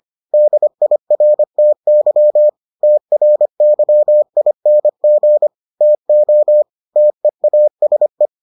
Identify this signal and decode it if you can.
Morse code, heard as a keyed tone, about 25 wpm, 600 Hz. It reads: DIRTY TRYING TO TEASE